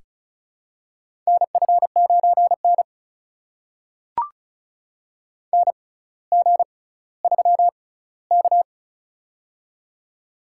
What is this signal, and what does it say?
Morse code, heard as a keyed tone, about 34 wpm, spaced as for 18 wpm, 700 Hz. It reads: NF9N NG3K
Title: USA Call Signs - 4 Characters - 6x Character Spacing - 35wpm